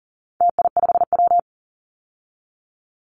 Morse code, heard as a keyed tone, about 40 wpm, 700 Hz. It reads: TI5W
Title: Call Signs - 40wpm